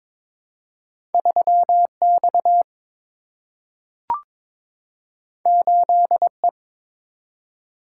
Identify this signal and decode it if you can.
Morse code, heard as a keyed tone, about 22 wpm, 700 Hz. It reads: 3X 8E
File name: practice / 22wpm / Two Letter-Number Combinations - 22wpm